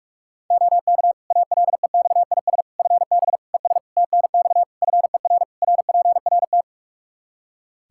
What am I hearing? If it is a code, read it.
Morse code, heard as a keyed tone, about 46 wpm, 700 Hz. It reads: OK ALEXIS FB ES TNX FER RPRT
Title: Practice QSO - Exchange 2 - Info - 45wpm